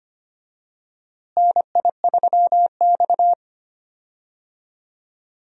Morse code, heard as a keyed tone, about 25 wpm, 700 Hz. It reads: NI3X
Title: USA Call Signs - 4 Characters - Repeated 2x - 25wpm